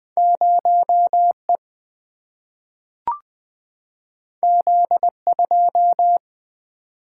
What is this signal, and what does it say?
Morse code, heard as a keyed tone, about 20 wpm, 700 Hz. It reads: 0 E Z 2
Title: Single-Letter Single-Number 20wpm